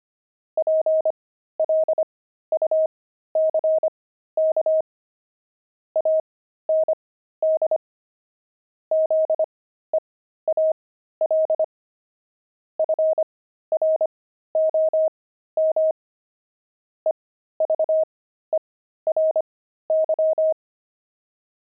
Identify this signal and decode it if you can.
Morse code, heard as a keyed tone, about 25 wpm, spaced as for 13 wpm, 630 Hz. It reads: PLUCK AND ZEAL FROM EVERY
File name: pangram-fs-13-cs-25-zeal